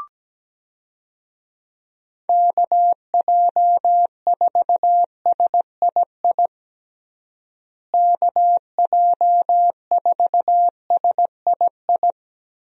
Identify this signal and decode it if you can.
Morse code, heard as a keyed tone, about 17 wpm, 700 Hz. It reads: KJ4SII KJ4SII
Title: USA Call Signs - 6 Characters - Repeated 2x - 17wpm